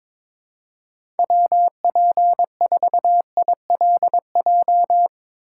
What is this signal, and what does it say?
Morse code, heard as a keyed tone, about 22 wpm, 700 Hz. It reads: WP4ILJ
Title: USA Call Signs - 6 Characters - 3x Character Spacing - 22wpm